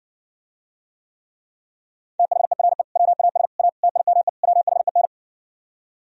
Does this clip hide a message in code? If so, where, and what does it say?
Yes: Morse code from 2.2 to 5.1 s, heard as a keyed tone, about 70 wpm, 700 Hz. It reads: THERE WAS A TIME WHEN